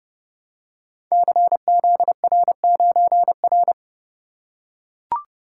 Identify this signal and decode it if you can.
Morse code, heard as a keyed tone, about 29 wpm, 700 Hz. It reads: CZR9R
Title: Code Groups - 5 Letter:Numbers - 30wpm